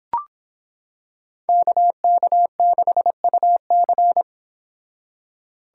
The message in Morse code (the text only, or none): KK6UC